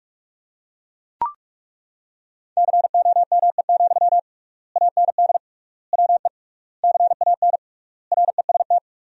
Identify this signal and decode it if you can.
Morse code, heard as a keyed tone, about 42 wpm, 700 Hz. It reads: COME, AND WE CAN REST
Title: 3x Word-Spacing - Sentences from Top 500 Words - 45wpm